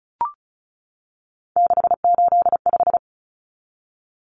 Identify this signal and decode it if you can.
Morse code, heard as a keyed tone, about 34 wpm, 700 Hz. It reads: 685
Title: Common Cut Numbers - 3-Digit 35wpm